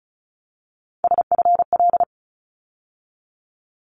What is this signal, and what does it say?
Morse code, heard as a keyed tone, about 34 wpm, 700 Hz. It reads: SFL
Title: ARRL Sections 35wpm